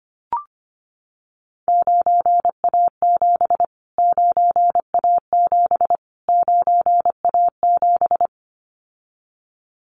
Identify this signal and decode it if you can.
Morse code, heard as a keyed tone, about 25 wpm, 700 Hz. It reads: 9A7 9A7 9A7
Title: Callsign Prefixes for Top 25 DXCC - Repeated 3x - 25wpm